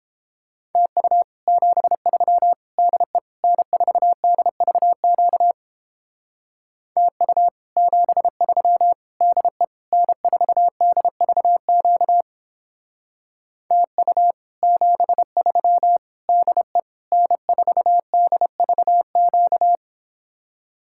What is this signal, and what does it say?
Morse code, heard as a keyed tone, about 29 wpm, 700 Hz. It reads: TU 73 DE N4DVQ TU 73 DE N4DVQ TU 73 DE N4DVQ